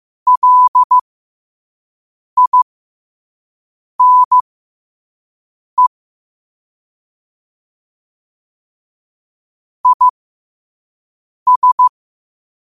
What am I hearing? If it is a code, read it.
Morse code, heard as a keyed tone, about 15 wpm, spaced as for 5 wpm, 1000 Hz. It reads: LINE IS
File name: qotdfiles / 5wpm